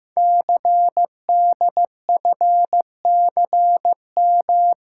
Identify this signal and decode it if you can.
Morse code, heard as a keyed tone, about 15 wpm, 700 Hz. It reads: CDFCM
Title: Code Groups - 5 Letters - 15wpm